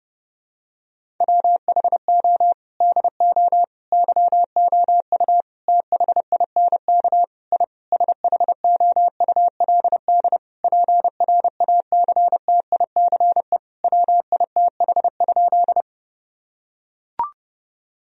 Morse code, heard as a keyed tone, about 30 wpm, 700 Hz. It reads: WHO DO YOU THINK I SHOULD PRACTICE WITH?
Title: Sentences from Top 1000 Words - Review Part 4 - 30wpm